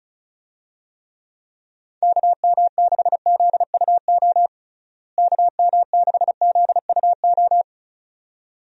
Morse code, heard as a keyed tone, about 36 wpm, 700 Hz. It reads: KM6ZUO KM6ZUO
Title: USA Call Signs - 6 Characters - Repeated 2x - 35wpm